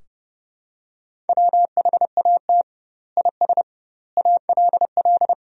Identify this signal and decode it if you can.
Morse code, heard as a keyed tone, about 32 wpm, 700 Hz. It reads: WHAT IS ALL
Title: 2x Word-Spacing - Sentences from Top 100 Words - 30wpm